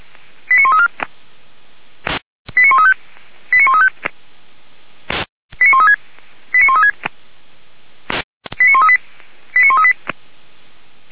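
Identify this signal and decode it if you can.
Five-tone selective call (ZVEI), heard as quick runs of five tones, tones of about 70 ms. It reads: 89134 89135 89135 89136 89136 89138 89138
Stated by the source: ZVEI five-tone call 89133 received by SDR, recorded by feuerwehr-orlamuende (F85360) fire brigade.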